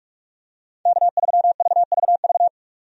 Morse code, heard as a keyed tone, about 46 wpm, 700 Hz. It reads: K3VVV
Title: USA Call Signs - 5 Characters - 45wpm